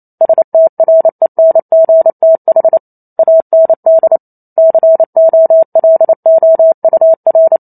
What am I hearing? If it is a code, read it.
Morse code, heard as a keyed tone, about 28 wpm, 650 Hz. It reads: STRENGTH AND COLOUR